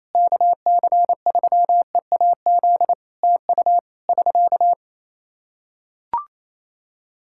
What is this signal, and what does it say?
Morse code, heard as a keyed tone, about 28 wpm, 700 Hz. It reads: KC3EAZ TU <SK>
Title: Practice QSO - Exchange 4 - Ending - 28wpm